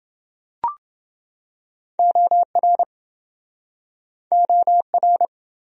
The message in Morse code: OR OR

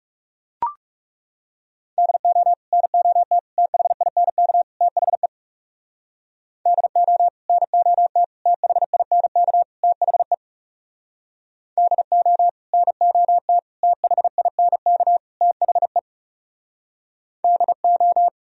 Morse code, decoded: DO NOT THINK THE DO NOT THINK THE DO NOT THINK THE DO

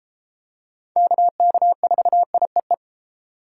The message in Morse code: KK4IEE

